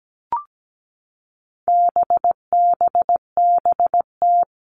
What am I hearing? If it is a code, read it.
Morse code, heard as a keyed tone, about 17 wpm, 700 Hz. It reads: BBBT
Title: All Cut Numbers - 4-Digit 17wpm